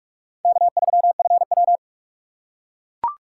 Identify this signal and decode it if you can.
Morse code, heard as a keyed tone, about 42 wpm, 700 Hz. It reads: K3FW